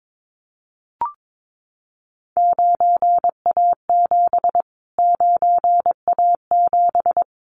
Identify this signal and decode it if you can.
Morse code, heard as a keyed tone, about 22 wpm, 700 Hz. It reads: 9A7 9A7
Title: Callsign Prefixes for Top 25 DXCC - Repeated 3x - 22wpm